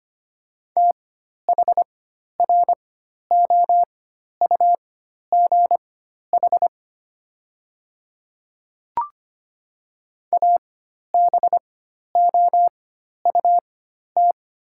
Morse code, heard as a keyed tone, about 25 wpm, spaced as for 12 wpm, 690 Hz. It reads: THROUGH ABOUT
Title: 4x Character Spacing - Top 300 Words - 25wpm